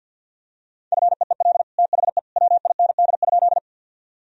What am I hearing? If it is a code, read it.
Morse code, heard as a keyed tone, about 50 wpm, 700 Hz. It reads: FEEL THE WIND?
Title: Sentences from Top 1000 Words - Review Part 2 - 50wpm